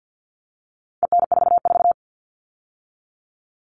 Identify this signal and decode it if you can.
Morse code, heard as a keyed tone, about 50 wpm, 700 Hz. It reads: EN44
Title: All Cut Numbers - 4-Digit 50wpm